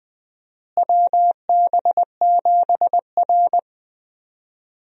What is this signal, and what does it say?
Morse code, heard as a keyed tone, about 20 wpm, 700 Hz. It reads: WB7R
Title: USA Call Signs - 4 Characters - Repeated 2x - 20wpm